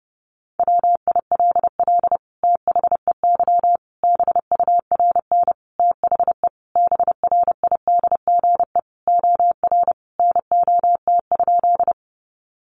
Morse code, heard as a keyed tone, about 30 wpm, 700 Hz. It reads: WILL THEY BURN THE BRIDGE OR NOT?
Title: Sentences from Top 1300 Words 30wpm